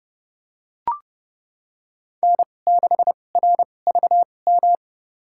Morse code, heard as a keyed tone, about 30 wpm, spaced as for 21 wpm, 700 Hz. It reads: N6RVM
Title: USA Call Signs - 5 Characters - 2x Character Spacing - 30wpm